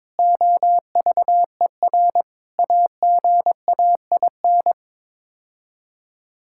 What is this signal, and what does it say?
Morse code, heard as a keyed tone, about 22 wpm, 700 Hz. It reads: OVER AGAIN
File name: practice / 22wpm / Sets of 6 Words - 22wpm